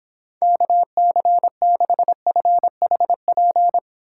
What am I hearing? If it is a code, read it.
Morse code, heard as a keyed tone, about 26 wpm, 700 Hz. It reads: KC6FHP